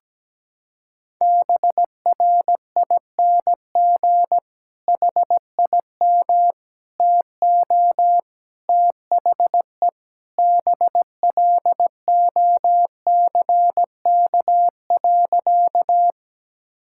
Morse code, heard as a keyed tone, about 17 wpm, 700 Hz. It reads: BRING HIM TO THE BLOCK.
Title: Sentences from Top 1600 Words 17wpm